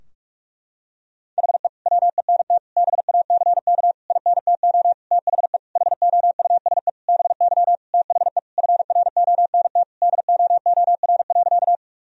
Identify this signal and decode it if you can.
Morse code, heard as a keyed tone, about 41 wpm, 700 Hz. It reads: HE WENT BACK INTO THE HOUSE BY THE FRONT DOOR.
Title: Sentences from Top 1000 Words - Review Part 2 - 45wpm